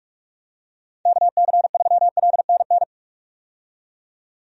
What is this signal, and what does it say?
Morse code, heard as a keyed tone, about 42 wpm, 700 Hz. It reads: KC3LNN
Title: USA Call Signs - 6 Characters - 45wpm